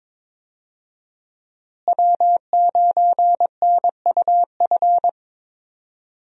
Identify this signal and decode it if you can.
Morse code, heard as a keyed tone, about 22 wpm, 700 Hz. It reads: W9NUF
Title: USA Call Signs - 5 Characters - Repeated 3x - 22wpm